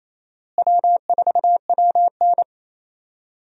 Morse code, heard as a keyed tone, about 28 wpm, 700 Hz. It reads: W4WN
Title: USA Call Signs - 4 Characters - Repeated 2x - 28wpm